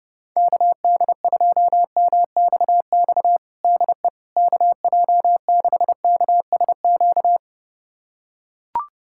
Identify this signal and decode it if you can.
Morse code, heard as a keyed tone, about 32 wpm, 700 Hz. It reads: KD2MXX DE KJ6KSQ
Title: USA Callsign DE Callsign 30wpm